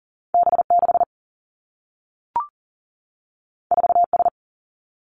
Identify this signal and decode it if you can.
Morse code, heard as a keyed tone, about 39 wpm, 700 Hz. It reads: B6 4S
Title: Two Challenging Letter-Number Combinations 40wpm